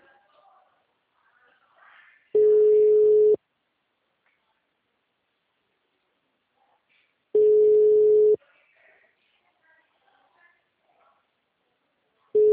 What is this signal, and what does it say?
Ringback tone (Europe, 425 Hz).
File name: out- -2014- - - .18